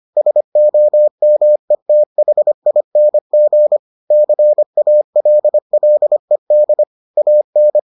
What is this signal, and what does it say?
Morse code, heard as a keyed tone, about 25 wpm, 600 Hz. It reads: SOMETHING CALLED AN